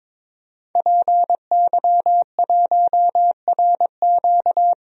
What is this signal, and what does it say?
Morse code, heard as a keyed tone, about 22 wpm, 700 Hz. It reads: PY1RQ
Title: DXCC-100 Call Signs - 5 Characters - 22wpm